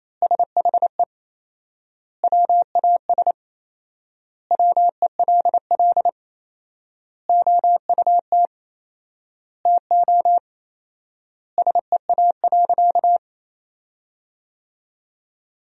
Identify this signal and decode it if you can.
Morse code, heard as a keyed tone, about 28 wpm, 700 Hz. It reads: SHE WAS WELL OUT TO SEA.
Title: 4x Word-Spacing - Sentences from Top 400 Words - 28wpm